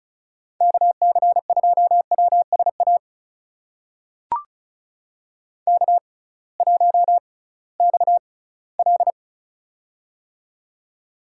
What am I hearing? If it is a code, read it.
Morse code, heard as a keyed tone, about 36 wpm, spaced as for 20 wpm, 700 Hz. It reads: KC2WSA K1XL
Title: USA Call Signs - 6x Character Spacing - 35wpm